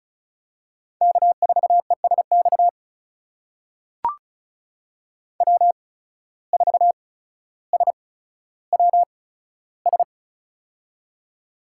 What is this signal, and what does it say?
Morse code, heard as a keyed tone, about 36 wpm, spaced as for 15 wpm, 700 Hz. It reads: K4ESX W4SWS
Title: USA Call Signs - 5 Characters - 8x Character Spacing - 35wpm